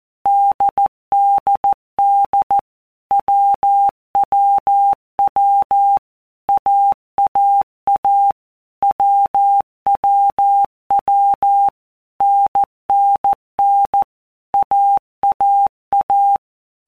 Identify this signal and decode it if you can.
Morse code, heard as a keyed tone, about 14 wpm, 800 Hz. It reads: DDD WWW AAA WWW NNN AAA
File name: Basic Letter Groups By Blocks / Groups@13 / AWUNDG@13